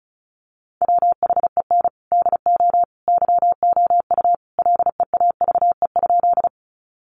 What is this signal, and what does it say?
Morse code, heard as a keyed tone, about 34 wpm, 700 Hz. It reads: WHEN DO YOU LEAVE?
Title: Sentences from Top 400 Words 35wpm